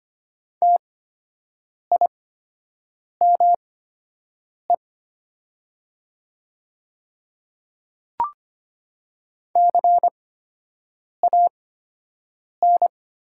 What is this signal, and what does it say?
Morse code, heard as a keyed tone, about 25 wpm, spaced as for 7 wpm, 700 Hz. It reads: TIME CAN
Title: 8x Character Spacing - Top 100 Words - 25wpm